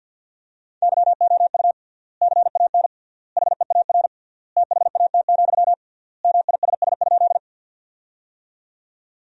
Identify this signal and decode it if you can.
Morse code, heard as a keyed tone, about 53 wpm, 700 Hz. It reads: YOU CAN HEAR THAT, MISS?
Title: 3x Word-Spacing - Sentences from Top 500 Words - 50wpm